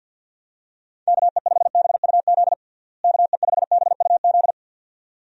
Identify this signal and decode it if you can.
Morse code, heard as a keyed tone, about 50 wpm, 700 Hz. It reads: KE5BUZ KE5BUZ
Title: USA Call Signs - 6 Characters - Repeated 2x - 50wpm